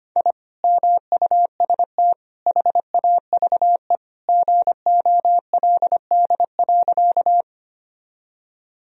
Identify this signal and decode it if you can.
Morse code, heard as a keyed tone, about 25 wpm, 700 Hz. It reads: I MUST HAVE GOLD.
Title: Sentences from Top 1000 Words - Review Part 3 - 25wpm